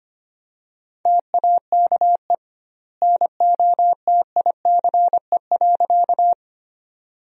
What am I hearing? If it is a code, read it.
Morse code, heard as a keyed tone, about 25 wpm, 700 Hz. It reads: TAKE NOTICE.
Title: 2x Word-Spacing - Sentences from Top 400 Words - 25wpm